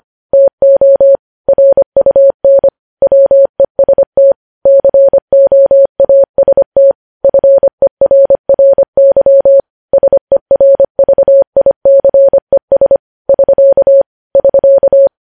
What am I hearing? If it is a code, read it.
Morse code, heard as a keyed tone, about 25 wpm, 550 Hz. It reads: TO RUN WEST COAST FERRY SERVICES <SK> <SK>
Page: news-headlines-in-morse-code-at-25-wpm